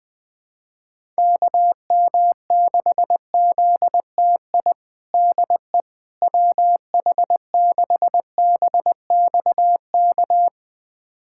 Morse code, heard as a keyed tone, about 20 wpm, 700 Hz. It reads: KM6ZTI DE WH6BXK
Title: USA Callsign DE Callsign - 6 Characters - 20wpm